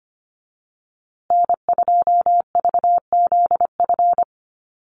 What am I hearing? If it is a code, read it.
Morse code, heard as a keyed tone, about 25 wpm, 700 Hz. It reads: N2VZF
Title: USA Call Signs Repeated 3x 25wpm